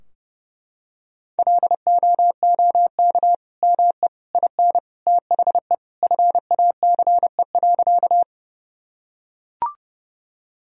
Morse code, heard as a keyed tone, about 32 wpm, 700 Hz. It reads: LOOK ME IN THE FACE.
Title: Sentences from Top 1000 Words - Review Part 2 - 30wpm